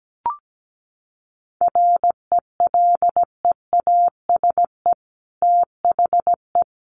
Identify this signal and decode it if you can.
Morse code, heard as a keyed tone, about 17 wpm, 700 Hz. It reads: RELEASE THE
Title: Sentences from Top 1100 Words 17wpm